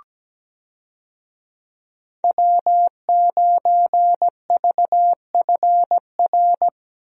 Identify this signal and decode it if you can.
Morse code, heard as a keyed tone, about 17 wpm, 700 Hz. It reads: W9VFR
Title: USA Call Signs - 5 Characters - Repeated 3x - 17wpm